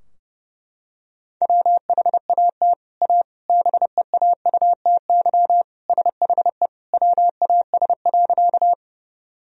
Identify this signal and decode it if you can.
Morse code, heard as a keyed tone, about 30 wpm, 700 Hz. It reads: WHAT A BEAUTY SHE WAS.
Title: Sentences from Top 1000 Words - Review Part 2 - 30wpm